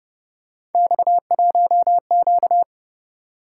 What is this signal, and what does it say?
Morse code, heard as a keyed tone, about 31 wpm, 700 Hz. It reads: X1Q
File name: Speed-Racing - Three Letter-Number Combinations - 25wpm